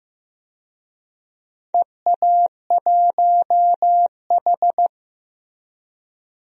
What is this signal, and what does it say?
Morse code, heard as a keyed tone, about 15 wpm, 700 Hz. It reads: EA1H
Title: DXCC-100 Call Signs - 4 Characters - Repeated 3x - 15wpm